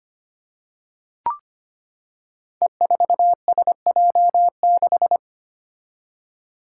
Morse code, heard as a keyed tone, about 25 wpm, 700 Hz. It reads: E4SJ6
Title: Five Challenging Letter-Number Combinations 25wpm